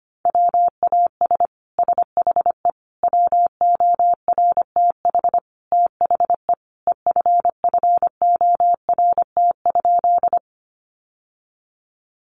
Morse code, heard as a keyed tone, about 25 wpm, 700 Hz. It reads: WAS SHE WORTH THE EFFORT?
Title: Sentences from Top 1200 Words 25wpm